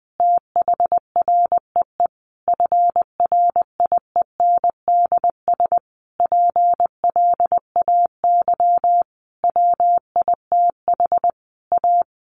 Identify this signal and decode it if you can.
Morse code, heard as a keyed tone, about 20 wpm, 700 Hz. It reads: THREE FRIENDS PLAY WITH A
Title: Easy Phrases 20wpm